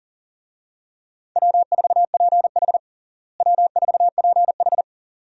Morse code, heard as a keyed tone, about 40 wpm, 700 Hz. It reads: W4PH W4PH
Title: USA Call Signs - 4 Characters - Repeated 2x - 40wpm